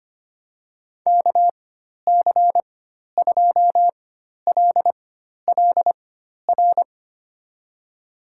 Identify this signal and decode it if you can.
Morse code, heard as a keyed tone, about 25 wpm, spaced as for 12 wpm, 700 Hz. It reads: KC2LLR
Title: USA Call Signs - 6 Characters - 4x Character Spacing - 25wpm